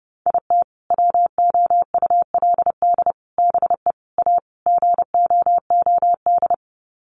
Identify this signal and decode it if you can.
Morse code, heard as a keyed tone, about 30 wpm, 700 Hz. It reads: IT WOULD BE A GOOD
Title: Sets of 5 Words Encore2 30wpm